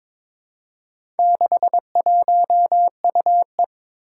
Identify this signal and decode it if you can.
Morse code, heard as a keyed tone, about 22 wpm, 700 Hz. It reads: 61UE
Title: Four Challenging Letter-Number Combinations 22wpm